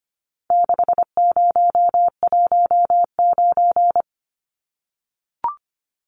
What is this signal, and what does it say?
Morse code, heard as a keyed tone, about 25 wpm, 700 Hz. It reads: 6019
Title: Numbers - 4-Digit 25wpm